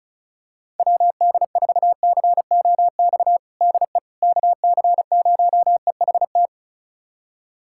Morse code, heard as a keyed tone, about 36 wpm, 700 Hz. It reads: WD4COX DE KC0EHT
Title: USA Callsign DE Callsign 35wpm